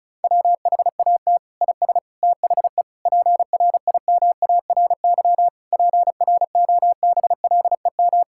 Morse code, heard as a keyed tone, about 36 wpm, 700 Hz. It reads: WHAT IS THE PRIMARY PROBLEM
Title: Sentences from Top 1000 Words - Review Part 5 - 35wpm